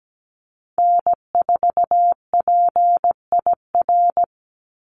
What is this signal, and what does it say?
Morse code, heard as a keyed tone, about 17 wpm, 700 Hz. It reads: N4PIR
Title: Call Signs Repeated 3x - 17wpm